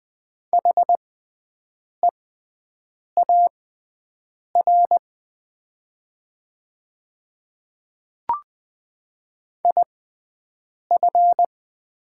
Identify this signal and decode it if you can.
Morse code, heard as a keyed tone, about 20 wpm, spaced as for 7 wpm, 700 Hz. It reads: HEAR IF
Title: 6x Character Spacing - Top 300 Words - 20wpm